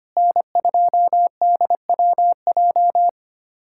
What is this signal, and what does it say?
Morse code, heard as a keyed tone, about 25 wpm, 700 Hz. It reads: N2DWJ
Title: USA Callsign DE Callsign - 5 Characters - 25wpm